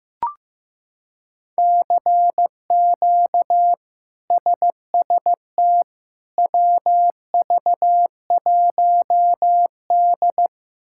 Morse code, heard as a keyed tone, about 15 wpm, 700 Hz. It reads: CQ SST WV1D
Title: K1USN SST QSO Practice - 15wpm